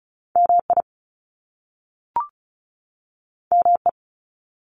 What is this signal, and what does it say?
Morse code, heard as a keyed tone, about 36 wpm, 700 Hz. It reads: MI ME
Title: US States 35wpm